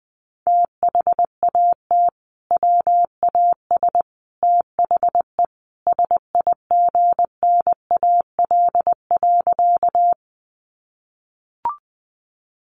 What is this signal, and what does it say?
Morse code, heard as a keyed tone, about 20 wpm, 700 Hz. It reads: THAT WAS THE SIGNAL.
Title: Sentences from Top 1400 Words 20wpm